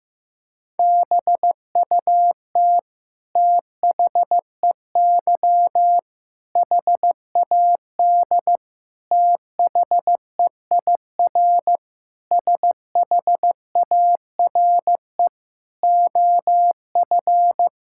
Morse code, heard as a keyed tone, about 15 wpm, 700 Hz. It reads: BUT THEY HAD THEIR SHARE OF
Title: Sentences from Top 1000 Words - Review Part 6 - 15wpm